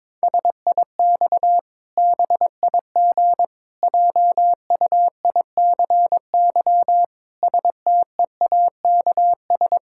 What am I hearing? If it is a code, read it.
Morse code, heard as a keyed tone, about 22 wpm, 700 Hz. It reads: SIX BIG JUICY STEAKS
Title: Pangram Challenge v2 - 22wpm